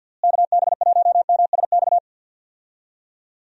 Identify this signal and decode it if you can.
Morse code, heard as a keyed tone, about 46 wpm, 700 Hz. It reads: KB1MSX
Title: USA Call Signs - 6 Characters - Repeated 3x - 50wpm